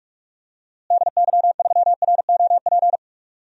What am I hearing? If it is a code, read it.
Morse code, heard as a keyed tone, about 46 wpm, 690 Hz. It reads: DY3ROP
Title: DXCC-100 Call Signs - 6 Characters - Repeated 2x - 45wpm